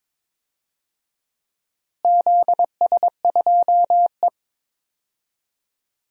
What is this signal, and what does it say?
Morse code, heard as a keyed tone, about 22 wpm, 700 Hz. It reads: ZS2E
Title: DXCC-100 Call Signs - 4 Characters - Repeated 2x - 22wpm